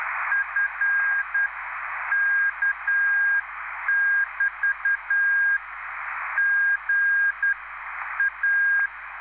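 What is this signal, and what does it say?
Morse code, heard as a keyed tone, about 10 wpm, spaced as for 9 wpm, 1600 Hz, with radio noise. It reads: F K<BT> G A
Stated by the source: Live CW QSO between PA2BX and SP3JFK on the 14055 kHz QRS frequency, at a basic 8 wpm.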